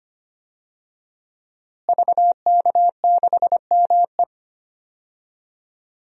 Morse code, heard as a keyed tone, about 25 wpm, 700 Hz. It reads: VK6ME
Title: DXCC-100 Call Signs - 5 Characters - Repeated 2x - 25wpm